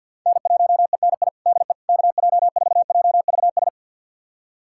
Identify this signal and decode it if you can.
Morse code, heard as a keyed tone, about 51 wpm, 690 Hz. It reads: N1ENI DE KJ4JVS
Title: USA Callsign DE Callsign 50wpm